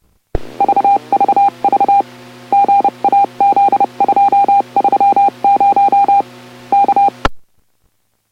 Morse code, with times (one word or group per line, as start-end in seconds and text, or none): 0.6-2.0 VVV
2.5-6.2 GAZ230
6.7-7.1 K